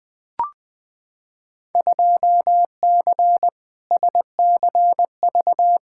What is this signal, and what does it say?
Morse code, heard as a keyed tone, about 20 wpm, 700 Hz. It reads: 2C SCV
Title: Field Day Exchanges Encore - 20wpm